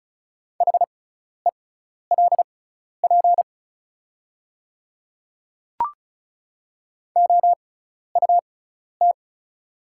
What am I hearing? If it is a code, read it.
Morse code, heard as a keyed tone, about 36 wpm, spaced as for 12 wpm, 700 Hz. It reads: HELP OUT
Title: 6x Character Spacing - Top 200 Words - 35wpm